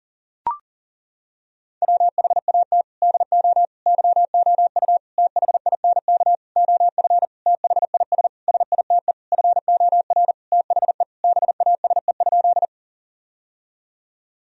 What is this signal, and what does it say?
Morse code, heard as a keyed tone, about 40 wpm, 700 Hz. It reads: WHAT DO YOU THINK OF THIS SITE FOR THE BASE?
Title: Sentences from Top 1000 Words - Review Part 1 - 40wpm